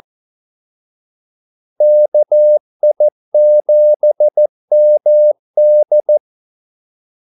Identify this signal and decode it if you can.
Morse code, heard as a keyed tone, about 14 wpm, 600 Hz. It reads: KI7MD
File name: Callsigns-14wpm